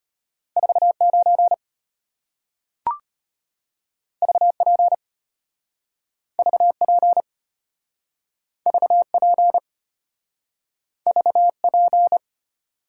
Morse code, heard as a keyed tone, about 34 wpm, 700 Hz. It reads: 49 VP VP VP VP